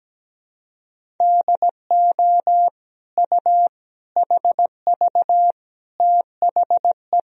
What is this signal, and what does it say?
Morse code, heard as a keyed tone, about 17 wpm, 700 Hz. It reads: DO U HV THE
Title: Sentences from Top 300 Words and Abbreviations 17wpm